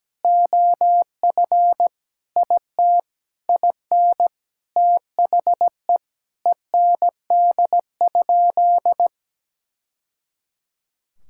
Morse code, heard as a keyed tone, about 17 wpm, 700 Hz. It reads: OF IT IN THE END?
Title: Sentences from Top 1000 Words - Review Part 5 - 17wpm